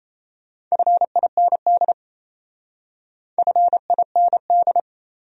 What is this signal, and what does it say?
Morse code, heard as a keyed tone, about 35 wpm, 700 Hz. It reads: FIND FIND